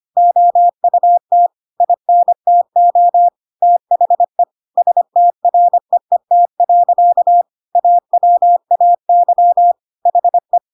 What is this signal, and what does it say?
Morse code, heard as a keyed tone, about 25 wpm, 700 Hz. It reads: OUT INTO THE STREET. AWAY HE
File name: Pinocchio / Pinocchio ch3 25 25 ik1hge